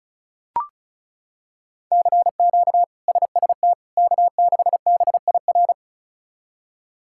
Morse code, heard as a keyed tone, about 34 wpm, 700 Hz. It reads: CQ SST K6BIR
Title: K1USN SST QSO Practice - 35wpm